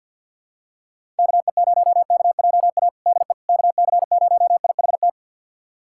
Morse code, heard as a keyed tone, about 48 wpm, 700 Hz. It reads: KE0KJA DE KC0IHT
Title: USA Callsign DE Callsign - 6 Characters - 50wpm